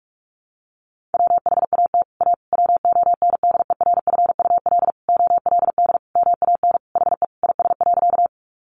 Morse code, heard as a keyed tone, about 44 wpm, 700 Hz. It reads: WHAT A WONDERFUL OLD MAN HE IS.
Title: Sentences from Top 1100 Words 45wpm